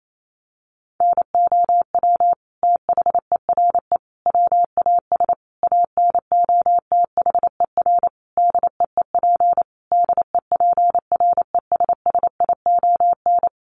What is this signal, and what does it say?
Morse code, heard as a keyed tone, about 28 wpm, 700 Hz. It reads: NOW THERE WAS ANOTHER DEEP DEPRESSION
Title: Sentences from Top 1300 Words 28wpm